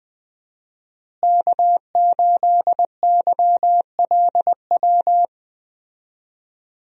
Morse code, heard as a keyed tone, about 20 wpm, 700 Hz. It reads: K8YLW
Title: USA Call Signs - 5 Characters - Repeated 3x - 20wpm